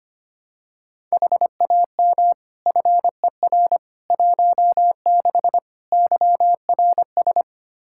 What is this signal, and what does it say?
Morse code, heard as a keyed tone, about 25 wpm, 700 Hz. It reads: HAM FER 16 YRS
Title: K1USN SST QSO Practice 25wpm